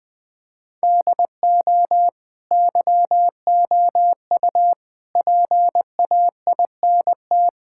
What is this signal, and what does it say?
Morse code, heard as a keyed tone, about 20 wpm, 700 Hz. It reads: DO YOU PAINT
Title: Sentences from Top 1000 Words - Review Part 6 - 20wpm